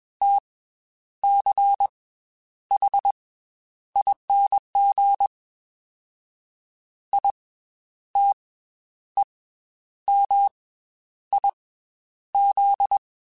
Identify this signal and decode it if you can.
Morse code, heard as a keyed tone, about 21 wpm, 800 Hz. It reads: T C H ING I T E M I Z
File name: Special INGs / Special INGs-mp3s / Slow ing @20